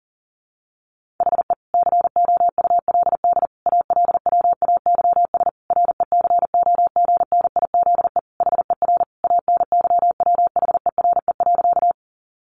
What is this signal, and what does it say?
Morse code, heard as a keyed tone, about 40 wpm, 700 Hz. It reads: HE COULD ALWAYS RECOGNIZE HER ANYWHERE.
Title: Sentences from Top 1400 Words 40wpm